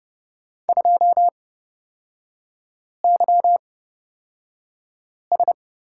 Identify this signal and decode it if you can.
Morse code, heard as a keyed tone, about 29 wpm, spaced as for 5 wpm, 700 Hz. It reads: 2YS
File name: Lesson 36e - BT - Three Characters 30wpm